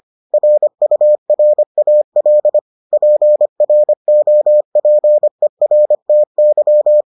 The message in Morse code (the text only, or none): RURAL PROPERTY